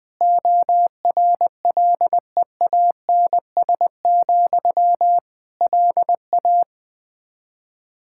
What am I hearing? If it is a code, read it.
Morse code, heard as a keyed tone, about 21 wpm, 700 Hz. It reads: ORLEANS, LA
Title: Practice QSO - Exchange 1 - Beginning - 20wpm